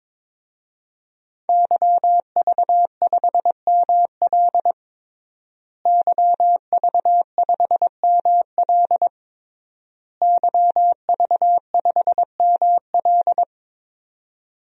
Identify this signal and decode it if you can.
Morse code, heard as a keyed tone, about 22 wpm, 700 Hz. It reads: YV5ML YV5ML YV5ML
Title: DXCC-100 Call Signs - 5 Characters - Repeated 3x - 22wpm